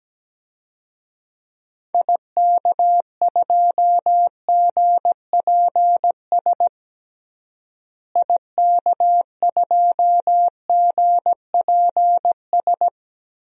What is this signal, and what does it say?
Morse code, heard as a keyed tone, about 17 wpm, 700 Hz. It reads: IK2GPS IK2GPS
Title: DXCC-100 Call Signs - 6 Characters - Repeated 2x - 17wpm